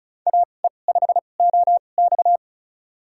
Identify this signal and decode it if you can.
Morse code, heard as a keyed tone, about 36 wpm, spaced as for 25 wpm, 700 Hz. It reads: AE5OX
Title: USA Call Signs - 5 Characters - 2x Character Spacing - 35wpm